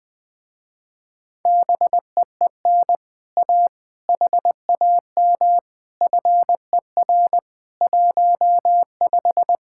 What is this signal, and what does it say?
Morse code, heard as a keyed tone, about 20 wpm, 700 Hz. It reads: BEEN A HAM FER 15
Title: Speed-Racing Common QSO Phrases - 17wpm